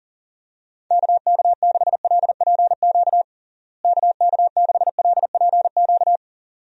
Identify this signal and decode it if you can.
Morse code, heard as a keyed tone, about 40 wpm, 700 Hz. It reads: KK6LPQ KK6LPQ
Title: USA Call Signs - 6 Characters - Repeated 2x - 40wpm